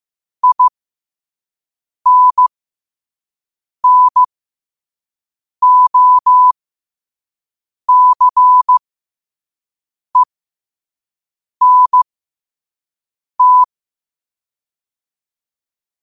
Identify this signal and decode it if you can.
Morse code, heard as a keyed tone, about 15 wpm, spaced as for 5 wpm, 1000 Hz. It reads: INNOCENT
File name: qotdfiles / 5wpm